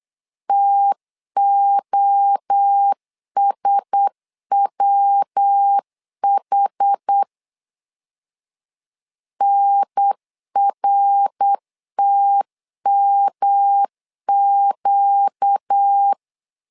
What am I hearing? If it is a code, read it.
Morse code, heard as a keyed tone, about 8 wpm, 800 Hz. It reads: TOSWH NRTMQ
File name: LEZIONI / 4 Q1S 8wpm 2710